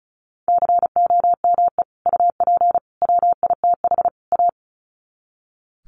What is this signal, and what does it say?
Morse code, heard as a keyed tone, about 34 wpm, 700 Hz. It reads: COME UP WITH A
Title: Sets of 4 Words 35wpm